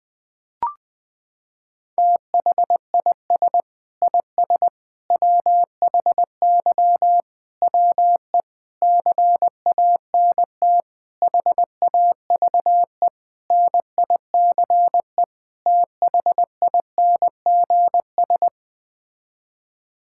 Morse code, heard as a keyed tone, about 20 wpm, 700 Hz. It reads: THIS IS WHY WE CANT HAVE NICE THINGS
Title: Sets of 8 Words - 20wpm